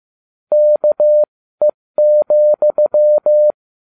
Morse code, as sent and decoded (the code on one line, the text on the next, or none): -.- . --..--
KE,